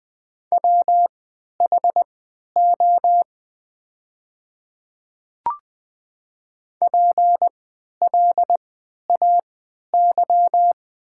Morse code, heard as a keyed tone, about 20 wpm, spaced as for 11 wpm, 700 Hz. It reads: WHO PLAY
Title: 3x Character Spacing - Top 200 Words - 20wpm